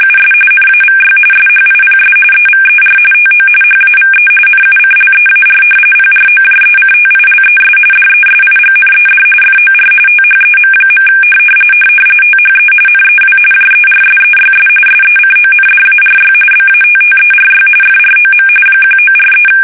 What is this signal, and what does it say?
RTTY, about 45.45 baud. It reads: US NAVY SHIPS EVER TO HAVE NUCLEAR PROJECTILES DESIGNED FOR THEM. THE NEW JERSEYS SECONDARY BATTERY FROM 1943
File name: fsk850 / battleshipnj